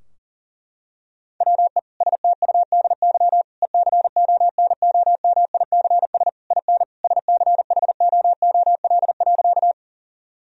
Morse code, heard as a keyed tone, about 40 wpm, 700 Hz. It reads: WE STUDY ECONOMICS IN SCHOOL.
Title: Sentences from Top 1000 Words - Review Part 3 - 40wpm